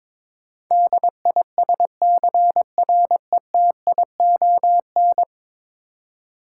Morse code, heard as a keyed tone, about 21 wpm, 700 Hz. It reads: DISCRETION
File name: practice / 22wpm / 10 Letter Words - First 3 Letters Given - 22wpm